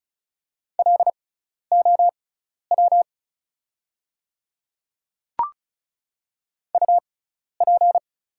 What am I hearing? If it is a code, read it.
Morse code, heard as a keyed tone, about 34 wpm, spaced as for 12 wpm, 690 Hz. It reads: LOW UP
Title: 6x Character Spacing - Top 200 Words - 35wpm